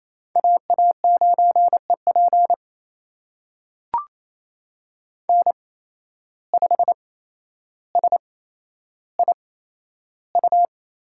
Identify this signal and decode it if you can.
Morse code, heard as a keyed tone, about 28 wpm, spaced as for 12 wpm, 700 Hz. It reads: AA9EP N5SIU
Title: USA Call Signs - 5 Characters - 8x Character Spacing - 28wpm